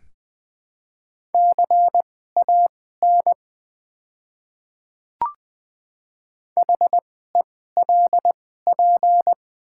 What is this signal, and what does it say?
Morse code, heard as a keyed tone, about 20 wpm, spaced as for 14 wpm, 700 Hz. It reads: CAN HELP
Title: 2x Character Spacing - Top 200 Words - 20wpm